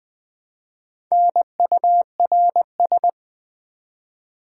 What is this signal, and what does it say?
Morse code, heard as a keyed tone, about 20 wpm, 700 Hz. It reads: NURS
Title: Build A Word - 1 Letter At A Time - 3 to 5-Letter Words - 20wpm